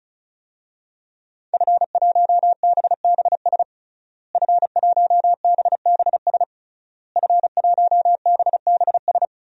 Morse code, heard as a keyed tone, about 36 wpm, 700 Hz. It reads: F1BBS F1BBS F1BBS
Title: DXCC-100 Call Signs - 5 Characters - Repeated 3x - 35wpm